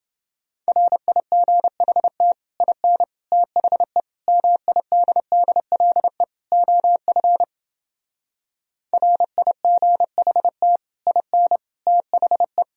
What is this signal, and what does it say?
Morse code, heard as a keyed tone, about 29 wpm, 700 Hz. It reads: RIGHT IN THE MIDDLE OF RIGHT IN THE